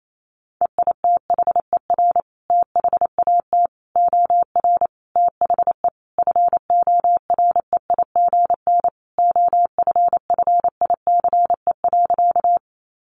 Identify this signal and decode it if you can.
Morse code, heard as a keyed tone, about 28 wpm, 700 Hz. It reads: EITHER THAT OR THE FOREIGN OFFICE.
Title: Sentences from Top 1400 Words 28wpm